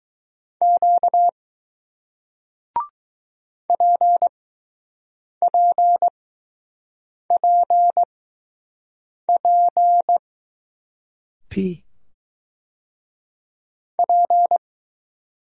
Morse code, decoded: Q P P P P P

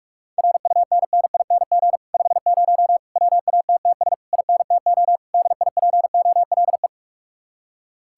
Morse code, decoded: RUNNING 50 WATTS INTO DIPOLE

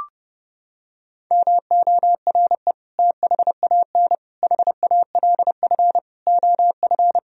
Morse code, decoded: MORE THAN HALF OF

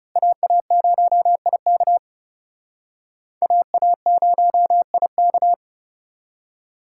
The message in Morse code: AA0IK AA0IK

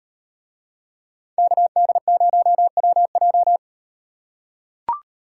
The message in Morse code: KD0WJ

